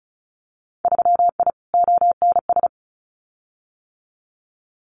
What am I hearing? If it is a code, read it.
Morse code, heard as a keyed tone, about 34 wpm, 700 Hz. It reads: 3I ONS
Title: Winter Field Day Exchanges Encore 35wpm